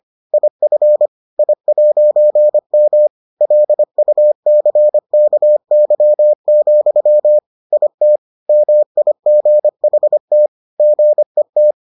Morse code, heard as a keyed tone, about 25 wpm, 600 Hz. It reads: IF I'M LUCKY, IT MIGHT GET